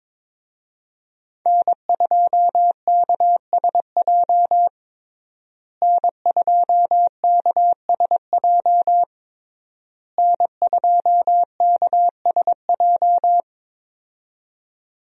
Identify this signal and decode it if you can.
Morse code, heard as a keyed tone, about 23 wpm, 700 Hz. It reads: N2KSJ N2KSJ N2KSJ
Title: USA Call Signs - 5 Characters - Repeated 3x - 22wpm